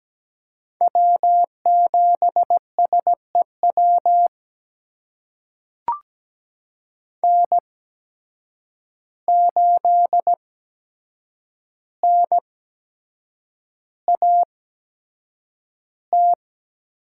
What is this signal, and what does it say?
Morse code, heard as a keyed tone, about 17 wpm, spaced as for 7 wpm, 700 Hz. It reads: W7SEW N8NAT
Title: USA Call Signs - 5 Characters - 8x Character Spacing - 17wpm